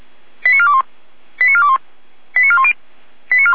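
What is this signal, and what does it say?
Five-tone selective call (ZVEI), heard as quick runs of five tones, tones of about 70 ms. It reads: 89421 89421 89410
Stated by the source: ZVEI five-tone selective call 89010-series code 89050 received on SDR.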